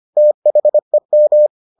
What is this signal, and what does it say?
Morse code, heard as a keyed tone, about 25 wpm, 600 Hz. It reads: THEM